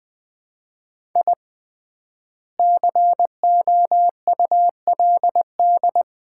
Morse code, heard as a keyed tone, about 20 wpm, 700 Hz. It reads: I COULD
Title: 3x Word-Spacing - Sets of 2 Words - 20wpm